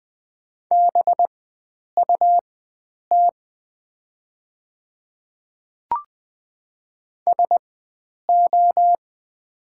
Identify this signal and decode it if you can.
Morse code, heard as a keyed tone, about 20 wpm, spaced as for 9 wpm, 690 Hz. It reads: BUT SO
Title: 4x Character Spacing - Top 100 Words - 20wpm